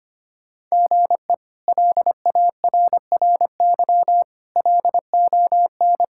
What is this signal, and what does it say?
Morse code, heard as a keyed tone, about 25 wpm, 700 Hz. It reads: GE LARRY LON
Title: K1USN SST QSO Practice - 25wpm